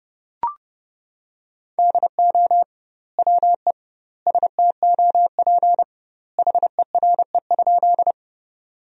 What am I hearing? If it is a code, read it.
Morse code, heard as a keyed tone, about 32 wpm, 700 Hz. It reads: DO WE STOP HERE?
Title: 2x Word-Spacing - Sentences from Top 300 Words - 30wpm